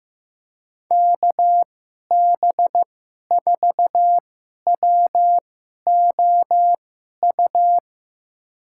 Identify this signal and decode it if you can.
Morse code, heard as a keyed tone, about 16 wpm, spaced as for 11 wpm, 700 Hz. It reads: KB4WOU
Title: USA Call Signs - 6 Characters - 2x Character Spacing - 15wpm